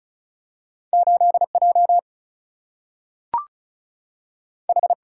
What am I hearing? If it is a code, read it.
Morse code, heard as a keyed tone, about 35 wpm, 700 Hz. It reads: 8 J H
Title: Single-Letter Single-Number 35wpm